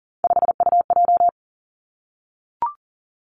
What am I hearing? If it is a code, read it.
Morse code, heard as a keyed tone, about 40 wpm, 700 Hz. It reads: 5UJ
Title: Three Challenging Letter-Number Combinations 40wpm